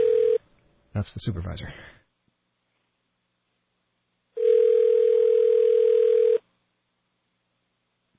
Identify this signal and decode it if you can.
Ringback tone (North America), 440 + 480 Hz, with speech.
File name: mp3files / off the hook